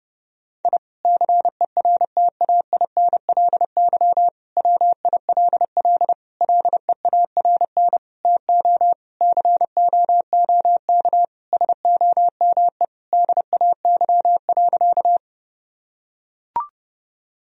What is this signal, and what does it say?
Morse code, heard as a keyed tone, about 30 wpm, 700 Hz. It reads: I CERTAINLY WILL LEARN TO COOK SOME DAY.
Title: Sentences from Top 1000 Words - Review Part 4 - 30wpm